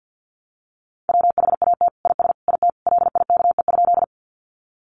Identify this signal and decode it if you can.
Morse code, heard as a keyed tone, about 50 wpm, 700 Hz. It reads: WHAT IS IT LIKE?
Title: Sentences from Top 100 Words 50wpm